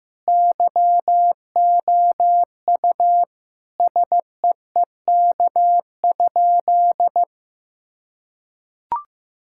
Morse code, heard as a keyed tone, about 15 wpm, 700 Hz. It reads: YOU SEEK?
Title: Sentences from Top 1600 Words 15wpm